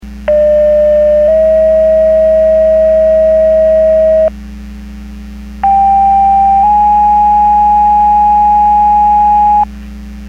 Two-tone page, heard as a one-second tone, then a three-second tone, about 600 then 640 Hz, about 790 then 830 Hz. It is heard from 0.3 to 9.6 s.
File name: Portage 154.385 Paging